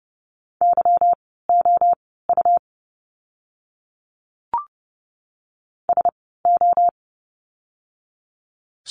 Morse code, heard as a keyed tone, about 30 wpm, spaced as for 17 wpm, 700 Hz. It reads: YOU SO
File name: course / 30:10wpm / 230 Lesson 37f - Letter x - Words 30wpm